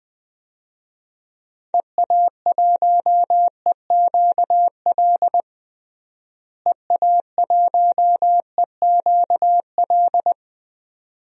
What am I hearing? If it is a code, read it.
Morse code, heard as a keyed tone, about 20 wpm, 690 Hz. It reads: EA1EQL EA1EQL
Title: DXCC-100 Call Signs - 6 Characters - Repeated 2x - 20wpm